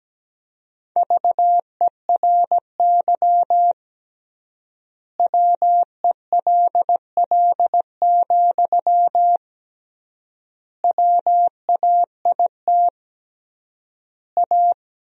Morse code, heard as a keyed tone, about 17 wpm, 700 Hz. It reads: VERY WELL, WAIT A
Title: 3x Word-Spacing - Sentences from Top 500 Words - 17wpm